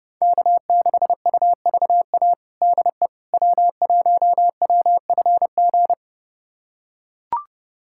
Morse code, heard as a keyed tone, about 32 wpm, 700 Hz. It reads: K6UVA DE W1WFG
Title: USA Callsign DE Callsign - 5 Characters - 30wpm